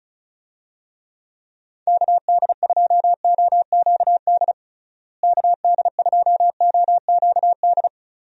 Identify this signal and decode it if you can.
Morse code, heard as a keyed tone, about 34 wpm, 700 Hz. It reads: KD2OQD KD2OQD
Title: USA Call Signs - 6 Characters - Repeated 3x - 35wpm